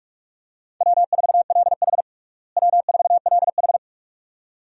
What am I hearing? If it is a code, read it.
Morse code, heard as a keyed tone, about 46 wpm, 700 Hz. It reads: W4LH W4LH
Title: USA Call Signs - 4 Characters - Repeated 2x - 45wpm